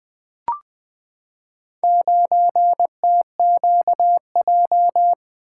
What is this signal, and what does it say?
Morse code, heard as a keyed tone, about 20 wpm, 700 Hz. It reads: 9TQJ
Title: Four Letter-Number Combinations - 20wpm